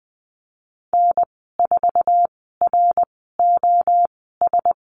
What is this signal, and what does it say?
Morse code, heard as a keyed tone, about 20 wpm, spaced as for 14 wpm, 700 Hz. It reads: N4ROS
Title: 069 Lesson 12g - Letter d - Call Signs 20wpm